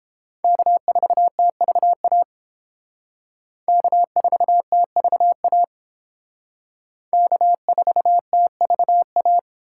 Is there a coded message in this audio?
Yes: Morse code from 0.4 to 9.4 s, heard as a keyed tone, about 30 wpm, 700 Hz. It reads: K4TVA K4TVA K4TVA